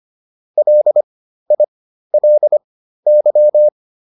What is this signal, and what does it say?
Morse code, heard as a keyed tone, about 25 wpm, spaced as for 13 wpm, 600 Hz. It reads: LILY